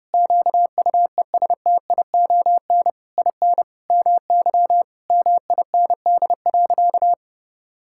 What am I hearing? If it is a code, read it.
Morse code, heard as a keyed tone, about 30 wpm, 700 Hz. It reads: QUESTION IN MY MIND.
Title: Sentences from Top 1000 Words - Review Part 6 - 30wpm